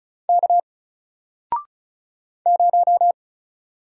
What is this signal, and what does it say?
Morse code, heard as a keyed tone, about 36 wpm, 700 Hz. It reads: K 0
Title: Single Letter-Number 35wpm